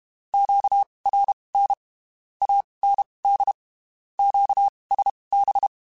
Morse code, heard as a keyed tone, about 32 wpm, spaced as for 23 wpm, 800 Hz. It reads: QRN AND QSB
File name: QSOs / QSOs-mp3s / WD0FBT 1@30